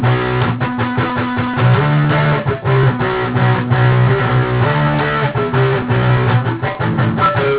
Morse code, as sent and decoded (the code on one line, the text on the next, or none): none